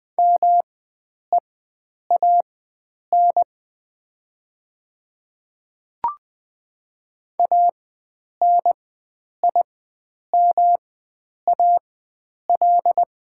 Morse code, heard as a keyed tone, about 20 wpm, spaced as for 9 wpm, 700 Hz. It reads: MEAN ANIMAL
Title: 4x Character Spacing - Top 500 Words - 20wpm